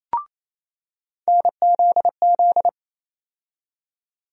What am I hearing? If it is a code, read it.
Morse code, heard as a keyed tone, about 28 wpm, 700 Hz. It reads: NZZ
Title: USA Call Sign Postfixes - 3 Letters 28wpm